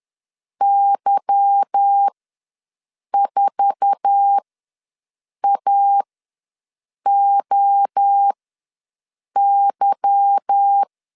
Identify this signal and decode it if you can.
Morse code, heard as a keyed tone, about 10 wpm, spaced as for 6 wpm, 800 Hz. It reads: Y4AOY
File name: LEZIONI / 8 YJ45 / 8 YJ4 ws 10wpm 4020